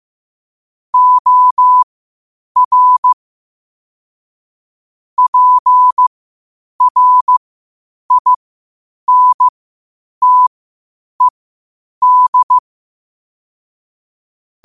Morse code, heard as a keyed tone, about 15 wpm, spaced as for 8 wpm, 1000 Hz. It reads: OR PRINTED